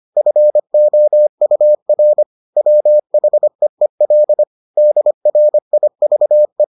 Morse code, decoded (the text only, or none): FOUR WHEEL DRIVE